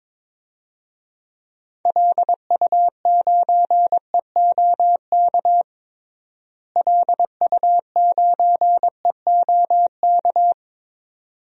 Morse code, heard as a keyed tone, about 22 wpm, 700 Hz. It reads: LU9EOK LU9EOK
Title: DXCC-100 Call Signs - 6 Characters - Repeated 2x - 22wpm